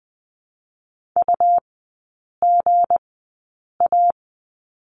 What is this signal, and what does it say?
Morse code, heard as a keyed tone, about 20 wpm, spaced as for 8 wpm, 690 Hz. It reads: UGA
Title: 172 Lesson 28e - Number 9 - Three Characters 20wpm